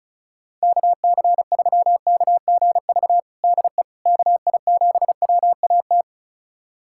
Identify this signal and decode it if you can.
Morse code, heard as a keyed tone, about 36 wpm, 700 Hz. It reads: KC3KGV DE KI7WAT
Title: USA Callsign DE Callsign 35wpm